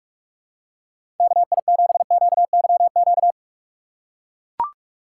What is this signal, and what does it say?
Morse code, heard as a keyed tone, about 42 wpm, 700 Hz. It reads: KI7QYQ
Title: USA Call Signs - 6 Characters - 4x Character Spacing - 45wpm